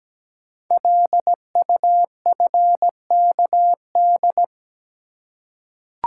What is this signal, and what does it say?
Morse code, heard as a keyed tone, about 17 wpm, 700 Hz. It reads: LUFKD
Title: Code Groups - 5 Letters - 17wpm